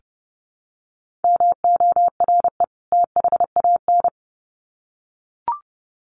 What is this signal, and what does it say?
Morse code, heard as a keyed tone, about 30 wpm, 700 Hz. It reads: MORE THAN
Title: Sets of 2 Words 30wpm